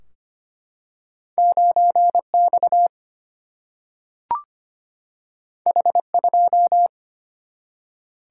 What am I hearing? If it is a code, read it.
Morse code, heard as a keyed tone, about 25 wpm, 700 Hz. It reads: 9X H2
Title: Single-Letter Single-Number 25wpm